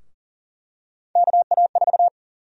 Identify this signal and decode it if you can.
Morse code, heard as a keyed tone, about 40 wpm, 700 Hz. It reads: KA4
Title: USA Call Sign Prefixes - 40wpm